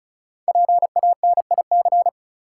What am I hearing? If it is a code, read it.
Morse code, heard as a keyed tone, about 36 wpm, 700 Hz. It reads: PANIC
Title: Five Letter Words - 35wpm